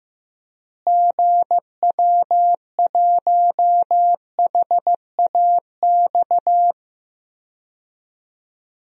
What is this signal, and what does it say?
Morse code, heard as a keyed tone, about 15 wpm, 700 Hz. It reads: GW1HAX
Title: DXCC-100 Call Signs - 6 Characters - Repeated 3x - 15wpm